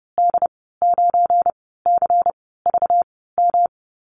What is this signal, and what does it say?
Morse code, heard as a keyed tone, about 30 wpm, 700 Hz. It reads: D 9 C V M
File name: Lesson 37g - Letter x - Call Signs 30wpm